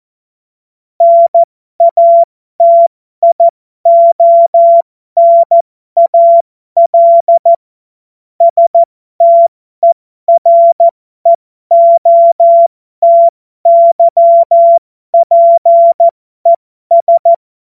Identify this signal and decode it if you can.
Morse code, heard as a keyed tone, about 14 wpm, spaced as for 12 wpm, 680 Hz. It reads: NATIONAL STEREOTYPES
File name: Morsekurs / Witze